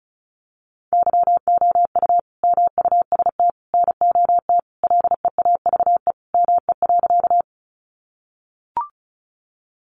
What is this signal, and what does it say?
Morse code, heard as a keyed tone, about 34 wpm, 700 Hz. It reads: YOU MUST NOT LEAVE ME.
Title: Sentences from Top 600 Words 35wpm